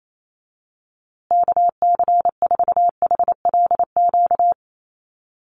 Morse code, heard as a keyed tone, about 28 wpm, 700 Hz. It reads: KC4HLQ
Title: USA Call Signs Repeated 3x 28wpm